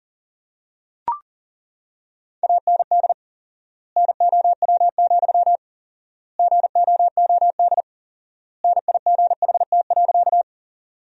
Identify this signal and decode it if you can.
Morse code, heard as a keyed tone, about 40 wpm, 700 Hz. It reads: AND NOW, GOOD NIGHT.
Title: 4x Word-Spacing - Sentences from Top 300 Words - 40wpm